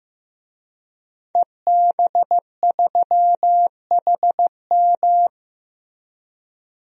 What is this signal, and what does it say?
Morse code, heard as a keyed tone, about 15 wpm, 700 Hz. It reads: EB3HM
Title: DXCC-100 Call Signs - 5 Characters - Repeated 2x - 15wpm